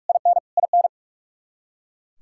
Morse code, heard as a keyed tone, about 44 wpm, 700 Hz. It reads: IN IN
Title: K1USN SST QSO Practice 45wpm